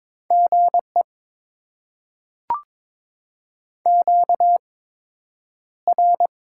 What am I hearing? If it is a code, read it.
Morse code, heard as a keyed tone, about 22 wpm, spaced as for 9 wpm, 700 Hz. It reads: GE QR
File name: Top 20 QSO Elements - 8x Character Spacing - 22wpm